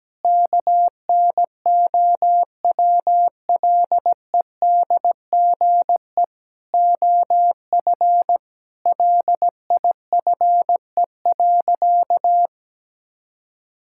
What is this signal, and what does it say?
Morse code, heard as a keyed tone, about 17 wpm, 700 Hz. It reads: KNOWLEDGE OF LIFE.
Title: Sentences from Top 1000 Words - Review Part 6 - 17wpm